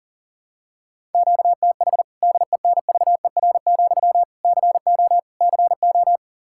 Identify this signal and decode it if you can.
Morse code, heard as a keyed tone, about 40 wpm, 700 Hz. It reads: QTH DENVER, CO CO
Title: Practice QSO - Exchange 1 - Beginning - 40wpm